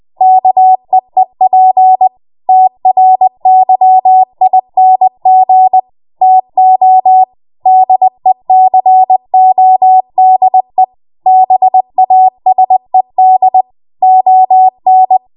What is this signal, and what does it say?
Morse code, heard as a keyed tone, about 20 wpm, 750 Hz. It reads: KEEP TRYING TO DECODE BASED ON
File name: Morse / 20WPM